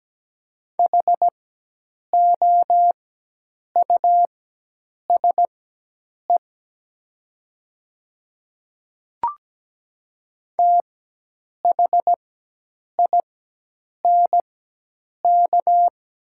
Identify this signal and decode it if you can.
Morse code, heard as a keyed tone, about 17 wpm, spaced as for 8 wpm, 700 Hz. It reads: HOUSE THINK
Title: 4x Character Spacing - Top 300 Words - 17wpm